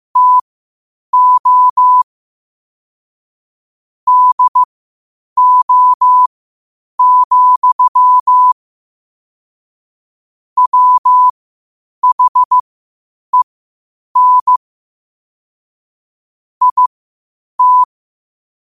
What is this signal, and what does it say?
Morse code, heard as a keyed tone, about 15 wpm, spaced as for 8 wpm, 1000 Hz. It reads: TO DO, WHEN IT
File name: qotdfiles / 8wpm